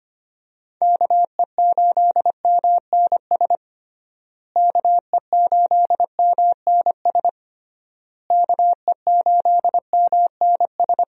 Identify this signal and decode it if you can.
Morse code, heard as a keyed tone, about 25 wpm, 700 Hz. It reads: KE8MNS KE8MNS KE8MNS
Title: USA Call Signs - 6 Characters - Repeated 3x - 25wpm